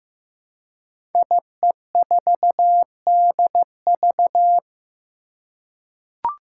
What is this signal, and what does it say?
Morse code, heard as a keyed tone, about 15 wpm, 700 Hz. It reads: IE4DV
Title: Five Challenging Letter-Number Combinations 15wpm